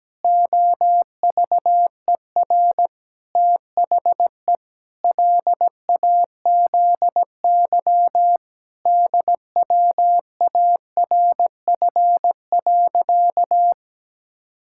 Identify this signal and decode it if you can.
Morse code, heard as a keyed tone, about 17 wpm, 700 Hz. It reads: OVER THE LAZY DWARF.
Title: Pangram Challenge v2 - 17wpm